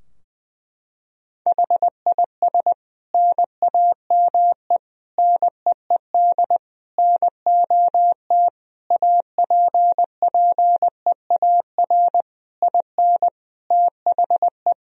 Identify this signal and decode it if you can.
Morse code, heard as a keyed tone, about 20 wpm, 700 Hz. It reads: HIS NAME NEED NOT APPEAR IN THE
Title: Sentences from Top 1000 Words - Review Part 2 - 20wpm